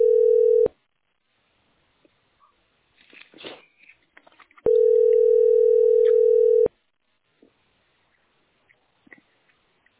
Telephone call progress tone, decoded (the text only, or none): ringback tone